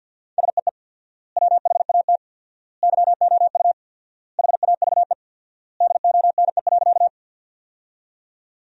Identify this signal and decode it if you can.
Morse code, heard as a keyed tone, about 51 wpm, 700 Hz. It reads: SEE WHAT YOU HAVE DONE.
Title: 4x Word-Spacing - Sentences from Top 500 Words - 50wpm